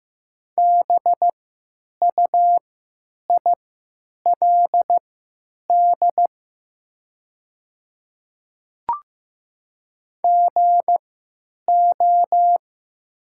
Morse code, heard as a keyed tone, about 15 wpm, spaced as for 9 wpm, 700 Hz. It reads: BUILD GO
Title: 3x Character Spacing - Top 200 Words - 15wpm